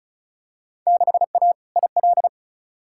Morse code, heard as a keyed tone, about 36 wpm, 700 Hz. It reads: 6A IL
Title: Field Day Exchanges - 35wpm